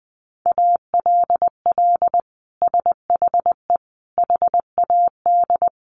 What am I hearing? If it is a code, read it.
Morse code, heard as a keyed tone, about 20 wpm, 700 Hz. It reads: ALL SHE HAD
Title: Sentences from Top 1100 Words 20wpm